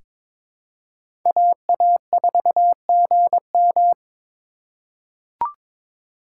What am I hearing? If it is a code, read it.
Morse code, heard as a keyed tone, about 21 wpm, 700 Hz. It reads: AA4GM
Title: USA Call Signs - 3x Character Spacing - 22wpm